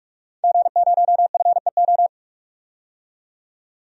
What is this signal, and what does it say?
Morse code, heard as a keyed tone, about 46 wpm, 700 Hz. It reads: G0FEO
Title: DXCC-100 Call Signs - 5 Characters - 45wpm